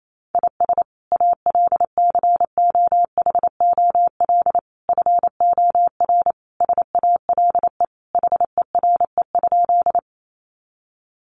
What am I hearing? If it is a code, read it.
Morse code, heard as a keyed tone, about 28 wpm, 700 Hz. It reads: IS ALCOHOL FOR SALE HERE?
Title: Sentences from Top 1400 Words 28wpm